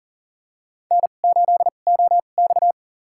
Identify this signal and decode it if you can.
Morse code, heard as a keyed tone, about 40 wpm, spaced as for 29 wpm, 700 Hz. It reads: N8OX
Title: USA Call Signs - 4 Characters - 2x Character Spacing - 40wpm